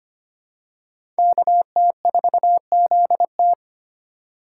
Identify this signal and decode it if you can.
Morse code, heard as a keyed tone, about 25 wpm, 700 Hz. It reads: KT4ZT
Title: USA Call Signs - 5 Characters - 2x Character Spacing - 25wpm